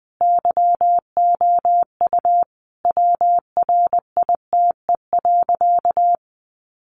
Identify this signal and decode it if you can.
Morse code, heard as a keyed tone, about 20 wpm, 700 Hz. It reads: YOU WRITE.
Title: Sentences from Top 800 Words 20wpm